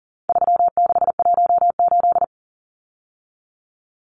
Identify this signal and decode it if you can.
Morse code, heard as a keyed tone, about 40 wpm, 700 Hz. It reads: 3618
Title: Numbers - 4-Digit 40wpm